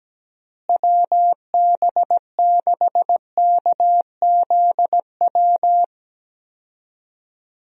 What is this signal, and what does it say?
Morse code, heard as a keyed tone, about 17 wpm, 700 Hz. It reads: WB6KZW
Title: USA Call Signs - 6 Characters - Repeated 3x - 17wpm